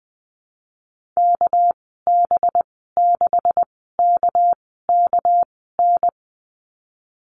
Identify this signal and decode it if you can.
Morse code, heard as a keyed tone, about 20 wpm, spaced as for 15 wpm, 700 Hz. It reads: KB6KKN
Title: 207 Lesson 33g - Letter j - Call Signs 20wpm